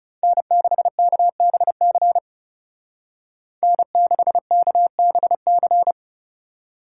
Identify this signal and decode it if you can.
Morse code, heard as a keyed tone, about 37 wpm, 700 Hz. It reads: N6KBC N6KBC
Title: Speed Racing - USA 1x3 Call Signs 30wpm